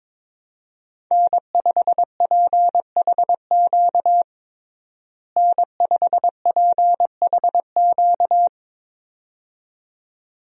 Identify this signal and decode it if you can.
Morse code, heard as a keyed tone, about 22 wpm, 700 Hz. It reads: N5PHQ N5PHQ
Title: USA Call Signs - 5 Characters - Repeated 2x - 22wpm